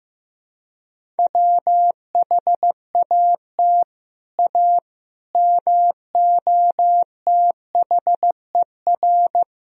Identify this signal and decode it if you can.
Morse code, heard as a keyed tone, about 15 wpm, 700 Hz. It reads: WHAT A MOTHER
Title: Sentences from Top 1000 Words - Review Part 1 - 15wpm